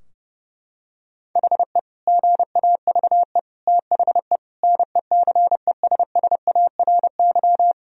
Morse code, heard as a keyed tone, about 30 wpm, 700 Hz. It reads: HE GAVE THE NECESSARY
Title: Sentences from Top 1000 Words - Review Part 2 - 30wpm